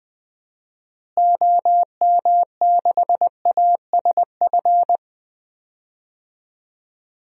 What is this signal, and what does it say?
Morse code, heard as a keyed tone, about 20 wpm, 700 Hz. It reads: OM6ASF
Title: DXCC-100 Call Signs - 6 Characters - Repeated 3x - 20wpm